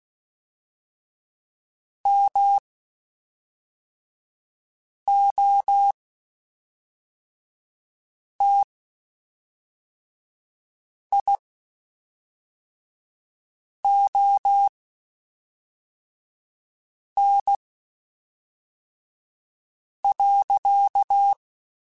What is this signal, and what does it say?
Morse code, heard as a keyed tone, about 16 wpm, spaced as for 3 wpm, 800 Hz. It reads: MOTION.
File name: Spaced / Chuckles Spaced-mp3s / BrainMouth sp @5